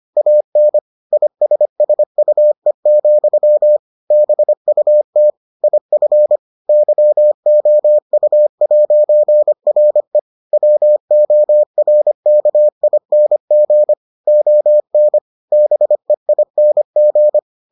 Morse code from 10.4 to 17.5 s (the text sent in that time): WORKING ON BEING